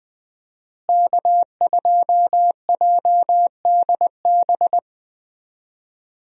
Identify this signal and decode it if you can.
Morse code, heard as a keyed tone, about 20 wpm, 700 Hz. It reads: K2JDB
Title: USA Call Signs - 5 Characters - Repeated 3x - 20wpm